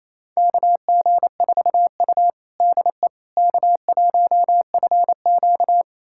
Morse code, heard as a keyed tone, about 28 wpm, 690 Hz. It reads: KG4U DE K1FQ
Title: USA Callsign DE Callsign - 4 Characters - 28wpm